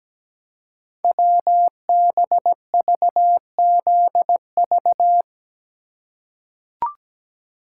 Morse code, heard as a keyed tone, about 17 wpm, 700 Hz. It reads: WBVZV
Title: Code Groups - 5 Letters - 17wpm